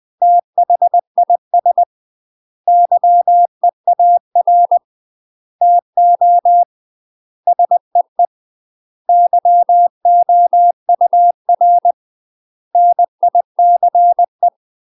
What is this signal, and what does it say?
Morse code, heard as a keyed tone, about 20 wpm, 700 Hz. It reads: THIS YEAR TO SEE YOUR NICE
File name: UK MASTER QSOs TRAINING / 20WPM / UK MASTER QSOs 0074